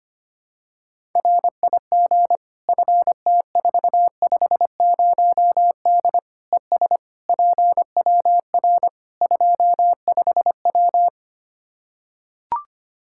Morse code, heard as a keyed tone, about 25 wpm, 700 Hz. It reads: RIG FT450D ES PWR 25W
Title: K1USN SST QSO Practice 25wpm